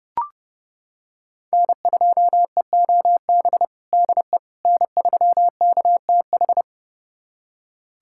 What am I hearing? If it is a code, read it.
Morse code, heard as a keyed tone, about 32 wpm, 700 Hz. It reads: N2EOB DE N3KTH
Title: USA Callsign DE Callsign - 5 Characters - 30wpm